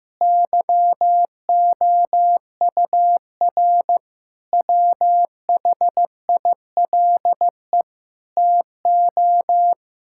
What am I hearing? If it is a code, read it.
Morse code, heard as a keyed tone, about 15 wpm, 700 Hz. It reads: YOUR WHILE TO
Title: Sentences from Top 1600 Words 15wpm